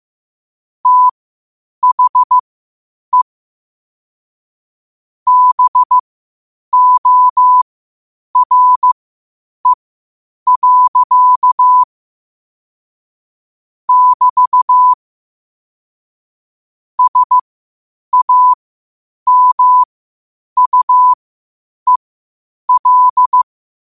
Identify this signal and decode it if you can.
Morse code, heard as a keyed tone, about 15 wpm, spaced as for 8 wpm, 1000 Hz. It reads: THE BORE. <BT> SAMUEL
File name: qotdfiles / 8wpm